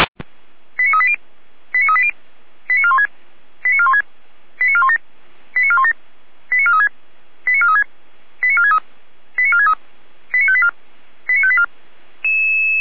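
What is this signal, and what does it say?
Five-tone selective call (ZVEI), heard as quick runs of five tones, tones of about 70 ms. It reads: 89380 89380 89416 89416 89417 89417 89436 89436 89463 89463 89584 89584
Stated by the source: ZVEI five-tone selective call received on SDR.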